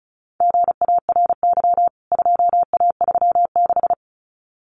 Morse code, heard as a keyed tone, about 34 wpm, 700 Hz. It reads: GARY 2A36
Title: CWT Exchange 35wpm